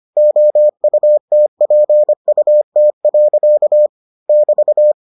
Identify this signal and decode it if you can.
Morse code, heard as a keyed tone, about 25 wpm, 600 Hz. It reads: OUTPUT. <BT>